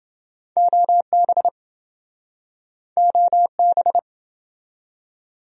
Morse code, heard as a keyed tone, about 28 wpm, 700 Hz. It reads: OB OB